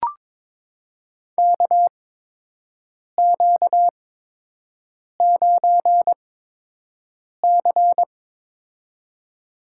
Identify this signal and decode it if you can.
Morse code, heard as a keyed tone, about 23 wpm, spaced as for 6 wpm, 700 Hz. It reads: KQ9C
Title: USA Call Signs - 4 Characters - 8x Character Spacing - 22wpm